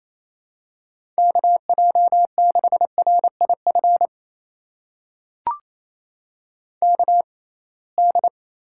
Morse code, heard as a keyed tone, about 27 wpm, spaced as for 21 wpm, 700 Hz. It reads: KJ6RIF KD
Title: USA Call Signs - 6 Characters - 6x Character Spacing - 28wpm